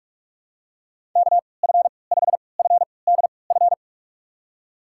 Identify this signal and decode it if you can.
Morse code, heard as a keyed tone, about 46 wpm, spaced as for 26 wpm, 700 Hz. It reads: KF5FDF
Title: USA Call Signs - 6 Characters - 3x Character Spacing - 45wpm